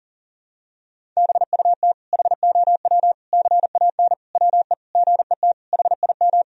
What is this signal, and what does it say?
Morse code, heard as a keyed tone, about 40 wpm, 700 Hz. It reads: BUT HOW CAN WE GET HIM
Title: Sentences from Top 1000 Words - Review Part 4 - 40wpm